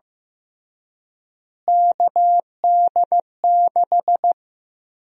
Morse code, heard as a keyed tone, about 15 wpm, 700 Hz. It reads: KD6
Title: USA Call Sign Prefixes - Repeated 2x - 15wpm